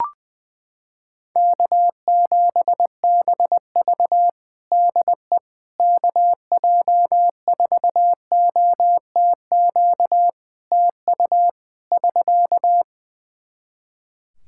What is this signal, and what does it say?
Morse code, heard as a keyed tone, about 19 wpm, 700 Hz. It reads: K7BV DE KJ4OTQ TU <SK>
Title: Practice QSO - Exchange 4 - Ending - 20wpm